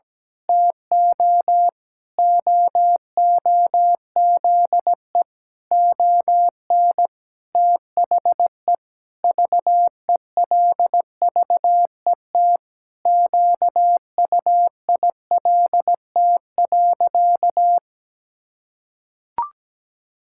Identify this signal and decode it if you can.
Morse code, heard as a keyed tone, about 17 wpm, 700 Hz. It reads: TO OOZE ON THE VELVET QUILT.
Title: Pangram Challenge v2 - 17wpm